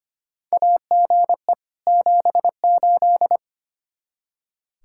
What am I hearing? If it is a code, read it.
Morse code, heard as a keyed tone, about 25 wpm, 700 Hz. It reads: AGE 78
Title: K1USN SST QSO Practice 25wpm